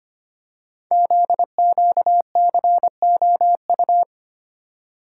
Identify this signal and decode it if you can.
Morse code, heard as a keyed tone, about 25 wpm, 700 Hz. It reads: ZQCOU
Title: Code Groups - 5 Letters - 25wpm